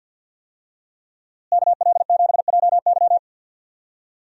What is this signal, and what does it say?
Morse code, heard as a keyed tone, about 52 wpm, 700 Hz. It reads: KL7JY
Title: DXCC-100 Call Signs - 5 Characters - Repeated 2x - 50wpm